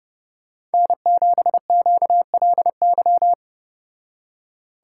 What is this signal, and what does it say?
Morse code, heard as a keyed tone, about 32 wpm, 700 Hz. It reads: N7QLY